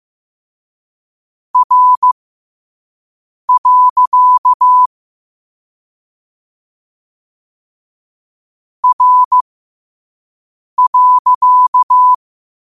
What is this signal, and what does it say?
Morse code, heard as a keyed tone, about 15 wpm, spaced as for 5 wpm, 1000 Hz. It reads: R. R.
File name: qotdfiles / 5wpm